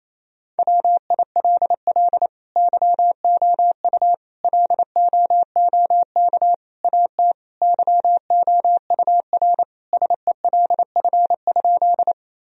Morse code, heard as a keyed tone, about 28 wpm, 700 Hz. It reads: WILL YOU LOOK AT YOUR SELF?
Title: Sentences from Top 1000 Words - Review Part 1 - 28wpm